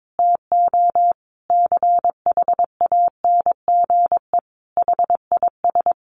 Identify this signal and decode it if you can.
Morse code, heard as a keyed tone, about 22 wpm, 700 Hz. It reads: TO CHANGE HIS
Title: Sentences from Top 1200 Words 22wpm